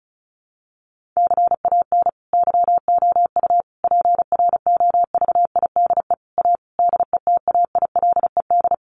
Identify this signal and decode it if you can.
Morse code, heard as a keyed tone, about 34 wpm, 700 Hz. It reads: CAN YOU PROVIDE A DETAILED
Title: Sentences from Top 1500 Words 35wpm